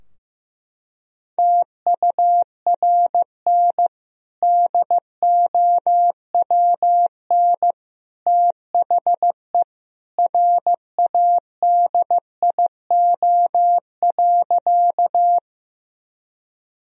Morse code, heard as a keyed tone, about 15 wpm, 700 Hz. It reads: TURN DOWN THE RADIO.
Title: Sentences from Top 1000 Words - Review Part 3 - 15wpm